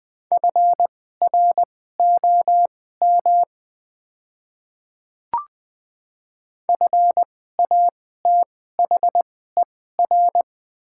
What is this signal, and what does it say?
Morse code, heard as a keyed tone, about 20 wpm, spaced as for 14 wpm, 700 Hz. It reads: FROM FATHER
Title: 2x Character Spacing - Top 200 Words - 20wpm